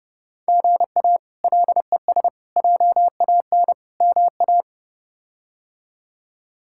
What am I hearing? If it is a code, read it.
Morse code, heard as a keyed tone, about 29 wpm, 700 Hz. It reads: GA LES JAN MA
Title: K1USN SST QSO Practice - 30wpm